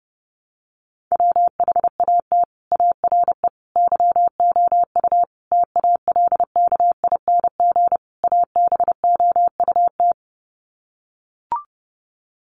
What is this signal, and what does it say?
Morse code, heard as a keyed tone, about 30 wpm, 700 Hz. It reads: WHAT ARE YOU TALKING ABOUT
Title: Sets of 5 Words 30wpm